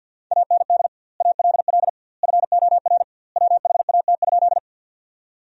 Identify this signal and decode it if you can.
Morse code, heard as a keyed tone, about 50 wpm, 700 Hz. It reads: AND ALL FOR WHAT?
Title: 2x Word-Spacing - Sentences from Top 100 Words - 50wpm